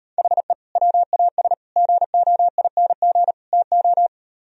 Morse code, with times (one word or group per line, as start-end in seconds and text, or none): 0.2-0.5 HE
0.8-1.5 WAS
1.8-3.3 GOING
3.5-4.1 TO